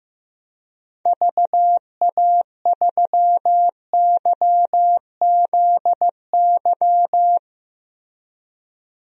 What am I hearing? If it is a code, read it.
Morse code, heard as a keyed tone, about 15 wpm, 700 Hz. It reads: VA3YZY
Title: DXCC-100 Call Signs - 6 Characters - Repeated 2x - 15wpm